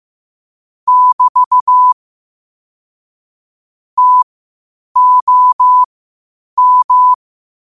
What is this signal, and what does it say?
Morse code, heard as a keyed tone, about 16 wpm, spaced as for 8 wpm, 1000 Hz. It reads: <BT> TOM
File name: qotdfiles / 8wpm